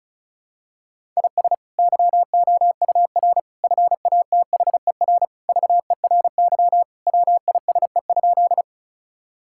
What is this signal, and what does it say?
Morse code, heard as a keyed tone, about 36 wpm, 700 Hz. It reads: IS YOUR FATHER VERY WISE?
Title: Sentences from Top 1600 Words 35wpm